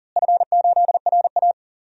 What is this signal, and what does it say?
Morse code, heard as a keyed tone, about 40 wpm, 700 Hz. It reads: F8RA
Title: DXCC-100 Call Signs - 4 Characters - Repeated 2x - 40wpm